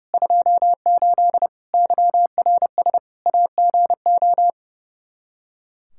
Morse code, heard as a keyed tone, about 30 wpm, 700 Hz. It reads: 28 YRS AGO
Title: Practice QSO - Exchange 3 - More Info - 30wpm